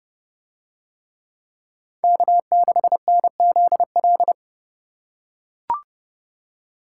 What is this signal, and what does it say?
Morse code, heard as a keyed tone, about 28 wpm, 700 Hz. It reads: K6NZL